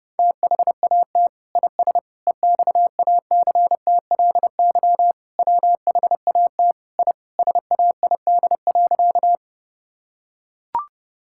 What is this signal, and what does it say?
Morse code, heard as a keyed tone, about 30 wpm, 700 Hz. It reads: THAT IS EXACTLY WHAT I SAID.
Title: Sentences from Top 1000 Words - Review Part 6 - 30wpm